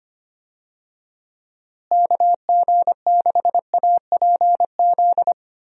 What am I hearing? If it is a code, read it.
Morse code, heard as a keyed tone, about 25 wpm, 700 Hz. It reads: KG6APZ
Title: USA Call Signs - 6 Characters - Repeated 3x - 25wpm